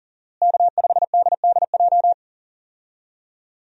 Morse code, heard as a keyed tone, about 40 wpm, 700 Hz. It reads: K5DDJ
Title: USA Call Signs - 5 Characters - Repeated 2x - 40wpm